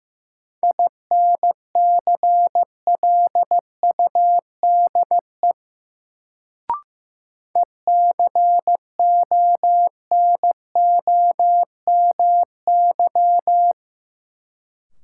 Morse code, heard as a keyed tone, about 15 wpm, 700 Hz. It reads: INCLUDE ECONOMY
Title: Sentences from Top 1000 Words - Review Part 3 - New Word Drill - 15wpm